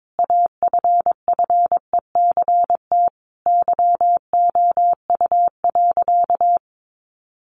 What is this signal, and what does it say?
Morse code, heard as a keyed tone, about 22 wpm, 700 Hz. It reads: AFFECT YOU.
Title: Sentences from Top 900 Words 22wpm